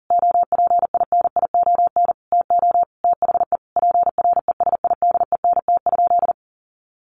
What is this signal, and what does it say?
Morse code, heard as a keyed tone, about 40 wpm, 700 Hz. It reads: OPINION TO THE PRESIDENT?
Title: Sentences from Top 1300 Words 40wpm